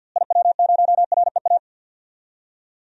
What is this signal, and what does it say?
Morse code, heard as a keyed tone, about 53 wpm, 700 Hz. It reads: IW0REA
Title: DXCC-100 Call Signs - 6 Characters - Repeated 2x - 50wpm